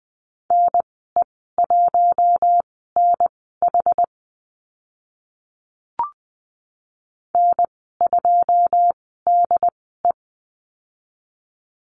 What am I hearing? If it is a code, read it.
Morse code, heard as a keyed tone, about 20 wpm, spaced as for 15 wpm, 700 Hz. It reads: NE1NH N2DE
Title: 082 Lesson 14g - Number 2 - Call Signs 20wpm